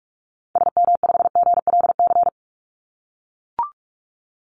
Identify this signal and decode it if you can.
Morse code, heard as a keyed tone, about 44 wpm, 700 Hz. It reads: SM5GLC
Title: Call Signs - 45wpm